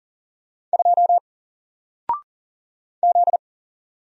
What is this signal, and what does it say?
Morse code, heard as a keyed tone, about 40 wpm, 700 Hz. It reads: 2 Z
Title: Single Letter-Number 40wpm